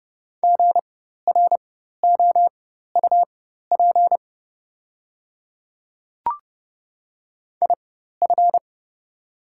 Morse code, heard as a keyed tone, about 30 wpm, spaced as for 14 wpm, 690 Hz. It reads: GROUP IF
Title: 4x Character Spacing - Top 300 Words - 30wpm